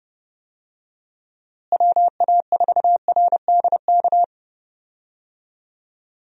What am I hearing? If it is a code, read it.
Morse code, heard as a keyed tone, about 30 wpm, 700 Hz. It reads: WA4RDK
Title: USA Call Signs - 6 Characters - Repeated 3x - 30wpm